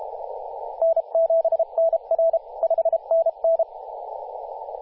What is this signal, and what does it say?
Morse code, heard as a keyed tone, about 32 wpm, 660 Hz, with radio noise. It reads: N7NR 5NN
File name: posts / 2016-oct-05-2124z-t31t-18-072mhz-cw